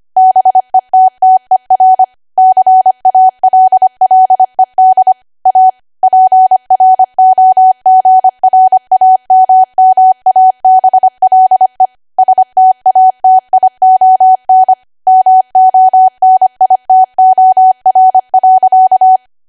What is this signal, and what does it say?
Morse code, heard as a keyed tone, about 25 wpm, 750 Hz. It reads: BETTER CALLED A PROGRAMMABLE STATION MONITOR.
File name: cwa / W1AW-Level3 / 25WPM